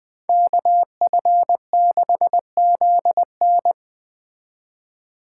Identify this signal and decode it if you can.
Morse code, heard as a keyed tone, about 20 wpm, 700 Hz. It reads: KF6ZN
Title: Speed Racing - USA Call Signs 17wpm